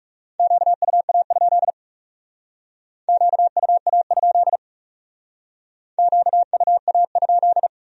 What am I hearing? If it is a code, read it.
Morse code, heard as a keyed tone, about 37 wpm, 700 Hz. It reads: QUA? QUA? QUA?
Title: Speed-Racing Q-Codes - 30wpm